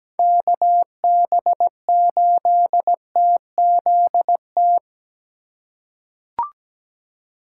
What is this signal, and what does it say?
Morse code, heard as a keyed tone, about 17 wpm, 700 Hz. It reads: KB8TZT
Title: USA Call Signs - 4x Character Spacing - 17wpm